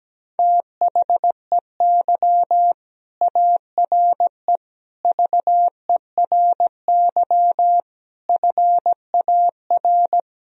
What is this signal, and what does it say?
Morse code, heard as a keyed tone, about 17 wpm, 700 Hz. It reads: THEY ARE VERY FAR
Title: Sentences from Top 1000 Words - Review Part 2 - 17wpm